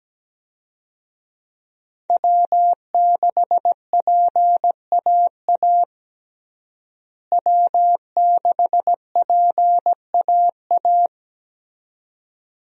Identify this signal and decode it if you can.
Morse code, heard as a keyed tone, about 17 wpm, 700 Hz. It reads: W6PAA W6PAA
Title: USA Call Signs - 5 Characters - Repeated 2x - 17wpm